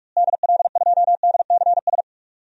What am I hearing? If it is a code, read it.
Morse code, heard as a keyed tone, about 46 wpm, 700 Hz. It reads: DL2DCS
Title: DXCC-100 Call Signs - 6 Characters - Repeated 3x - 45wpm